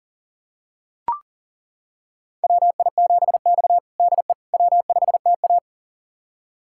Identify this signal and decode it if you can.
Morse code, heard as a keyed tone, about 37 wpm, 700 Hz. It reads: WI7X DE W5TA
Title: USA Callsign DE Callsign - 4 Characters - 40wpm